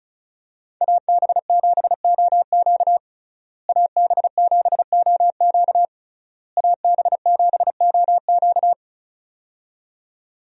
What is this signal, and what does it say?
Morse code, heard as a keyed tone, about 34 wpm, 700 Hz. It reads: AB7OQ AB7OQ AB7OQ
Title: USA Call Signs - 5 Characters - Repeated 3x - 35wpm